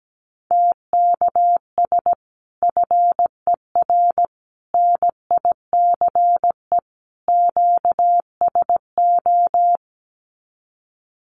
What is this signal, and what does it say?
Morse code, heard as a keyed tone, about 17 wpm, 700 Hz. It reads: TKS FER NICE QSO
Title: Common QSO Phrases 17wpm